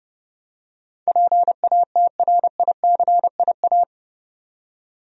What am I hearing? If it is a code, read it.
Morse code, heard as a keyed tone, about 30 wpm, 700 Hz. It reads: PATRICIA
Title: USA Common Names - 30wpm